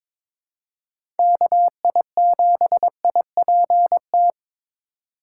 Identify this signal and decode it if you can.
Morse code, heard as a keyed tone, about 22 wpm, 700 Hz. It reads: KI7IPT
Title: USA Call Signs - 3x Character Spacing - 22wpm